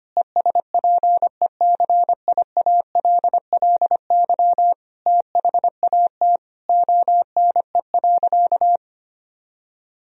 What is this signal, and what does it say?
Morse code, heard as a keyed tone, about 25 wpm, 700 Hz. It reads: ESPECIALLY THAT ONE.
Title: Sentences from Top 1000 Words - Review Part 4 - 25wpm